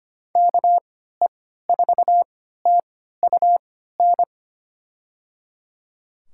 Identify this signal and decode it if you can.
Morse code, heard as a keyed tone, about 25 wpm, spaced as for 14 wpm, 700 Hz. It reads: KE4TUN
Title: USA Call Signs - 6 Characters - 3x Character Spacing - 25wpm